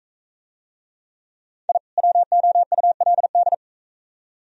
Morse code, heard as a keyed tone, about 41 wpm, 690 Hz. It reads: I WOULD